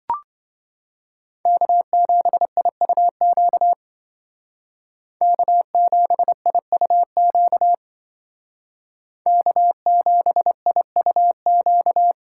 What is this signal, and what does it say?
Morse code, heard as a keyed tone, about 25 wpm, 700 Hz. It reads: K7IUQ K7IUQ K7IUQ